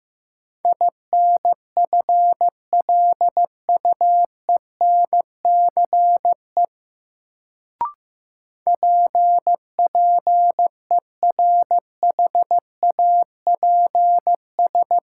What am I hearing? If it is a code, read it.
Morse code, heard as a keyed tone, about 15 wpm, 700 Hz. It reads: INFLUENCE PPERHAPS
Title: Sentences from Top 1100 Words - New Word Drill Alt - 15wpm